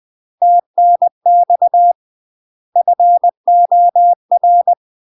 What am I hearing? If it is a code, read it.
Morse code, heard as a keyed tone, about 20 wpm, 700 Hz. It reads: TNX FOR
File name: UK MASTER QSOs TRAINING / 20WPM / UK MASTER QSOs 0032